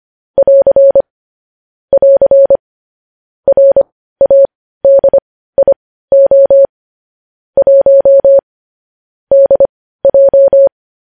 Morse code, heard as a keyed tone, about 25 wpm, spaced as for 15 wpm, 550 Hz. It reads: <AR> <AR> RADIO 1 DJ
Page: news-headlines-in-morse-code-at-15-wpm